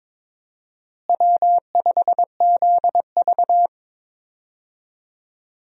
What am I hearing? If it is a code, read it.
Morse code, heard as a keyed tone, about 22 wpm, 700 Hz. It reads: W5ZV
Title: USA Call Signs - 4 Characters - Repeated 2x - 22wpm